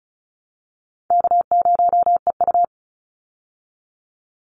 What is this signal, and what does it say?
Morse code, heard as a keyed tone, about 34 wpm, 700 Hz. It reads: K0EU
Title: Easy Call Signs Repeated 2x - 35wpm